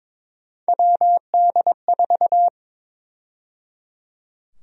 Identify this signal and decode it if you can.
Morse code, heard as a keyed tone, about 22 wpm, 700 Hz. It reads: WD4
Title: USA Call Sign Prefixes - 22wpm